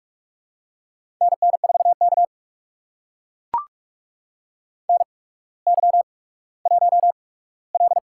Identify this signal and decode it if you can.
Morse code, heard as a keyed tone, about 44 wpm, spaced as for 19 wpm, 700 Hz. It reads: NN4K NY1L
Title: USA Call Signs - 4 Characters - 8x Character Spacing - 45wpm